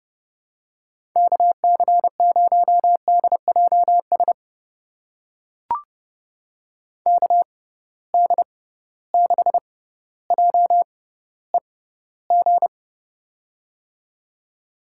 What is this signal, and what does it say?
Morse code, heard as a keyed tone, about 30 wpm, spaced as for 15 wpm, 700 Hz. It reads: KC0DJS KD6JEG
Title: USA Call Signs - 6 Characters - 6x Character Spacing - 30wpm